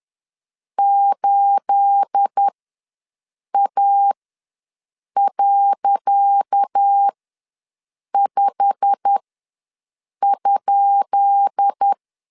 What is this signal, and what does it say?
Morse code, heard as a keyed tone, about 10 wpm, spaced as for 6 wpm, 800 Hz. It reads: 8A.5?
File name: LEZIONI / 10 SIGNS / 10C DOT ws 10wpm 4213